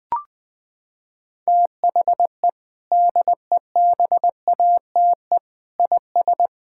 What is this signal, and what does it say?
Morse code, heard as a keyed tone, about 21 wpm, 700 Hz. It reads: THE DEBATE IS
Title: Sentences with 6-Word Phrases - 20wpm